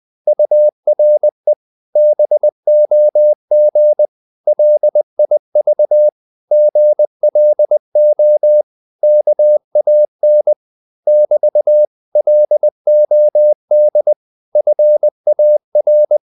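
Morse code, heard as a keyed tone, about 20 wpm, 600 Hz. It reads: URE BOG LIV GLO KAN <BT> LOD FAR
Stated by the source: Morse code practice of short Danish words at 20 wpm, from CW med Gnister.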